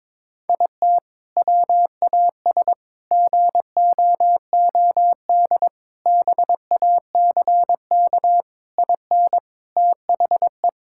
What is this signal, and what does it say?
Morse code, heard as a keyed tone, about 22 wpm, 700 Hz. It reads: IT WAS GOOD BACK IN THE
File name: practice / 22wpm / Sentences with 3-Word Phrases - 22wpm